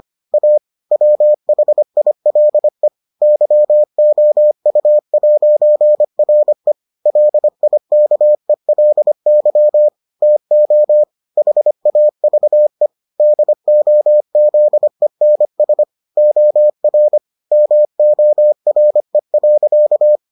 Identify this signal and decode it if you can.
Morse code, heard as a keyed tone, about 25 wpm, 600 Hz. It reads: A WHILE YOU'RE LIKELY TO HAVE DOZENS OR MORE.